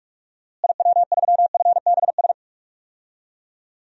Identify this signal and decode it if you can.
Morse code, heard as a keyed tone, about 46 wpm, 700 Hz. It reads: IW3FBS
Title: DXCC-100 Call Signs - 6 Characters - Repeated 2x - 45wpm